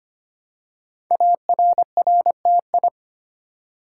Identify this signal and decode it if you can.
Morse code, heard as a keyed tone, about 25 wpm, 700 Hz. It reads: ARRTI
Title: Code Groups - 5 Letters - 25wpm